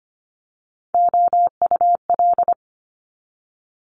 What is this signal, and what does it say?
Morse code, heard as a keyed tone, about 25 wpm, 700 Hz. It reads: OUL
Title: Multi-letter Phonemes 25wpm